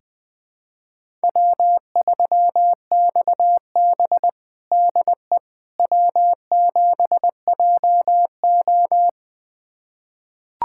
Morse code, heard as a keyed tone, about 20 wpm, 700 Hz. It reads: W3XB DE W7JO
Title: USA Callsign DE Callsign - 4 Characters - 20wpm